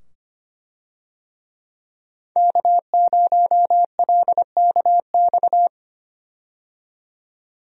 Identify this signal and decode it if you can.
Morse code, heard as a keyed tone, about 25 wpm, 700 Hz. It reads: K0LKX
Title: USA Call Signs - 5 Characters - 25wpm